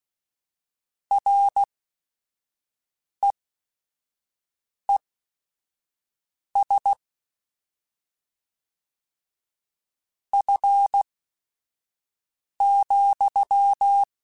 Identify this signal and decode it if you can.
Morse code, heard as a keyed tone, about 16 wpm, spaced as for 5 wpm, 800 Hz. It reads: REES F,
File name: Weather Sets / Weather Sets-mp3s / Weather 5 5 @5